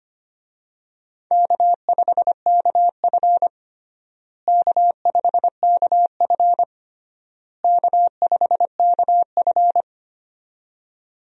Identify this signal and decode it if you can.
Morse code, heard as a keyed tone, about 25 wpm, 700 Hz. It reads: K5KF K5KF K5KF
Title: USA Call Signs - 4 Characters - Repeated 3x - 25wpm